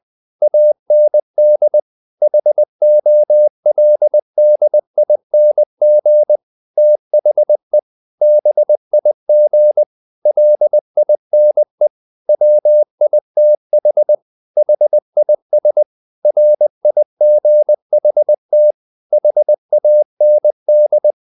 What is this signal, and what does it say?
Morse code, heard as a keyed tone, about 20 wpm, 600 Hz. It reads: AND HOLDING THE BIG LINE WITH HIS RIGHT HAND
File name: the old man and the sea part 2 20wpm